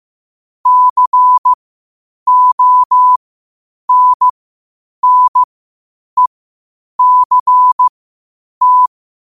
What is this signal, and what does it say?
Morse code, heard as a keyed tone, about 15 wpm, spaced as for 8 wpm, 1000 Hz. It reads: CONNECT
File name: qotdfiles / 8wpm